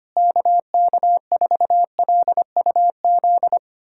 Morse code, heard as a keyed tone, about 25 wpm, 700 Hz. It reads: KK4LUZ
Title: USA Callsign DE Callsign 25wpm